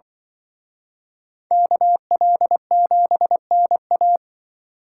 Morse code, heard as a keyed tone, about 24 wpm, 700 Hz. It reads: KL7NA
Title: Speed Racing - USA 2x2 Call Signs 20wpm